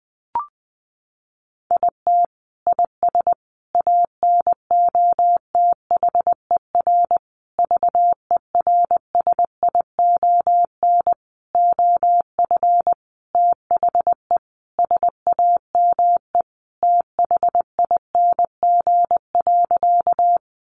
Morse code, heard as a keyed tone, about 19 wpm, 700 Hz. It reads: IT IS ANOTHER VERSION OF THE SAME THING.
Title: Sentences from Top 1400 Words 20wpm